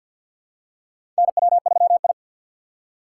Morse code, heard as a keyed tone, about 51 wpm, 700 Hz. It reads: NW3I
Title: USA Call Signs - 4 Characters - 2x Character Spacing - 50wpm